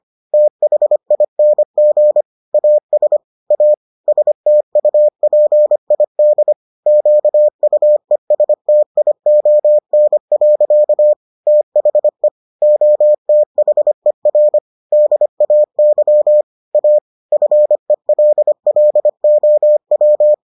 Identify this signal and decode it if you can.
Morse code, heard as a keyed tone, about 25 wpm, 600 Hz. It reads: THING AS A STUPID QUESTION. THE OTHER DAY A FELLOW